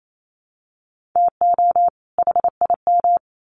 Morse code, heard as a keyed tone, about 28 wpm, 700 Hz. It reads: TO HIM
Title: Sets of 2 Words 28wpm